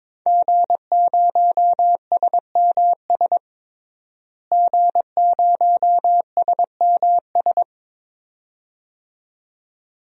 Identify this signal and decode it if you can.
Morse code, heard as a keyed tone, about 22 wpm, 700 Hz. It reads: G0SMS G0SMS
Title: DXCC-100 Call Signs - 5 Characters - Repeated 2x - 22wpm